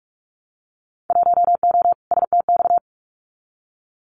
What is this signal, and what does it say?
Morse code, heard as a keyed tone, about 44 wpm, 700 Hz. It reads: 1O STX
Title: Winter Field Day Exchanges 45wpm